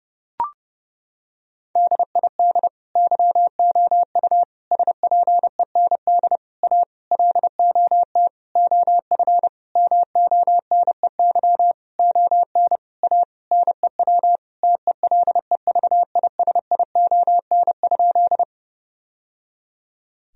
Morse code, heard as a keyed tone, about 30 wpm, 700 Hz. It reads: DID YOU SPEND A LOT OF MONEY ON A NEW TELEVISION?
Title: Sentences from Top 1000 Words - Review Part 4 - 30wpm